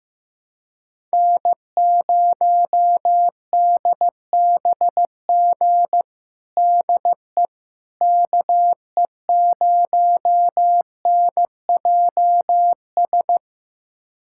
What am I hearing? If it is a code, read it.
Morse code, heard as a keyed tone, about 16 wpm, 700 Hz. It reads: N0DBG DE KE0NJS
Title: USA Callsign DE Callsign 15wpm